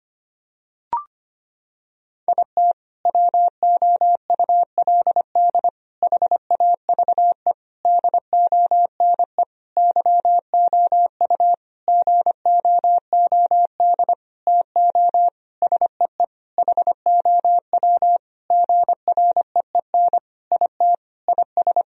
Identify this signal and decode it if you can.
Morse code, heard as a keyed tone, about 25 wpm, 700 Hz. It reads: IT WOULD HAVE DONE YOU GOOD TO SEE HOW GREEN IT IS
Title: Sentences from Top 1000 Words - Review Part 2 - 25wpm